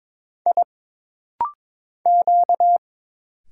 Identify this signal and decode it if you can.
Morse code, heard as a keyed tone, about 23 wpm, 700 Hz. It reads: I Q
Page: This rapid-fire timing is geared to push your ICR (Instant Character Recognition) skills.